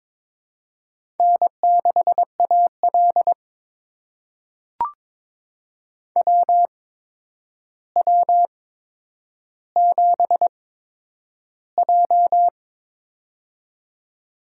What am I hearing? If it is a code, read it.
Morse code, heard as a keyed tone, about 23 wpm, spaced as for 10 wpm, 700 Hz. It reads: N6AL WW7J
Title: USA Call Signs - 4 Characters - 8x Character Spacing - 22wpm